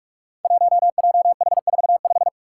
Morse code, heard as a keyed tone, about 42 wpm, 700 Hz. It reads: 1JH45
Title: Five Challenging Letter-Number Combinations 45wpm